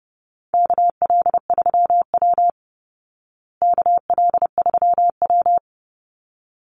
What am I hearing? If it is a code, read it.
Morse code, heard as a keyed tone, about 30 wpm, 700 Hz. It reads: KL3W KL3W
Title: Call Signs Repeated 3x - 30wpm